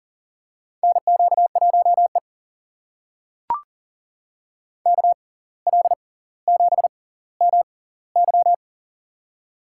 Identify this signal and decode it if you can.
Morse code, heard as a keyed tone, about 40 wpm, spaced as for 19 wpm, 700 Hz. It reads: NQ1E KL7MY
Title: USA Call Signs - 6x Character Spacing - 40wpm